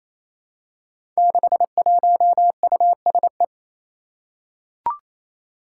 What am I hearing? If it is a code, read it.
Morse code, heard as a keyed tone, about 28 wpm, 700 Hz. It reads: 61USE
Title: Five Challenging Letter-Number Combinations 28wpm